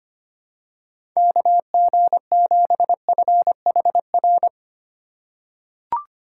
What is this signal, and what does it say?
Morse code, heard as a keyed tone, about 25 wpm, 700 Hz. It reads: KG7FHR
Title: USA Call Signs - 6 Characters - 2x Character Spacing - 25wpm